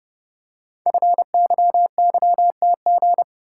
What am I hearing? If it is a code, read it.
Morse code, heard as a keyed tone, about 30 wpm, 700 Hz. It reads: FYYTG
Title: Code Groups - 5 Letters - 30wpm